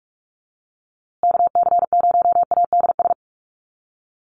Morse code, heard as a keyed tone, about 44 wpm, 700 Hz. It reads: KC0ADS
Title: USA Call Signs Repeated 2x 45wpm